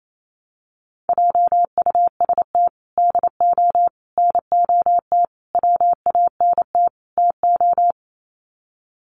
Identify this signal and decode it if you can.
Morse code, heard as a keyed tone, about 28 wpm, 700 Hz. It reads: JUST DO NOT WANT TO
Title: Sets of 5 Words Encore 28wpm